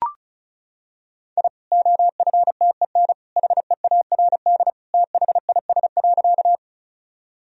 Morse code, heard as a keyed tone, about 36 wpm, 700 Hz. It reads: I OFTEN HEARD THIS.
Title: Sentences from Top 1000 Words - Review Part 1 - 35wpm